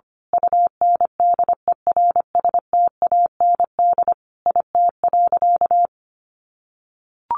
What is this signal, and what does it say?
Morse code, heard as a keyed tone, about 25 wpm, 700 Hz. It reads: UNDERSTAND IT.
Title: Sentences from Top 1400 Words 25wpm